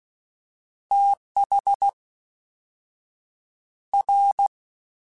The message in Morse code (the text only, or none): TH R